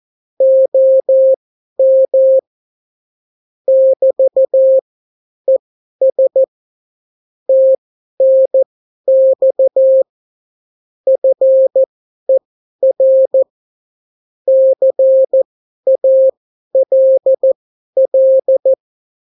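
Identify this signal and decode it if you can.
Morse code, heard as a keyed tone, about 14 wpm, spaced as for 11 wpm, 530 Hz. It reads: OM <BT> ES TNX FER CALL
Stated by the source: Example Spanish CW QSO from a Morse learning book, 14 wpm characters with 8 wpm spacing.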